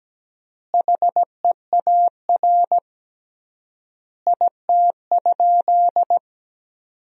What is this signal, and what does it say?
Morse code, heard as a keyed tone, about 17 wpm, 700 Hz. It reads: HEAR IT?
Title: 3x Word-Spacing - Sentences from Top 300 Words - 17wpm